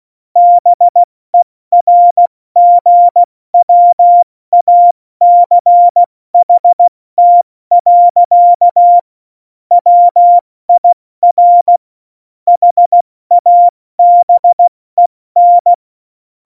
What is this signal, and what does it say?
Morse code, heard as a keyed tone, about 16 wpm, 700 Hz. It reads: BERGWACHT. WIR HABEN
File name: Morsekurs / Witze / Witz 28. Januar 2020, 70 F80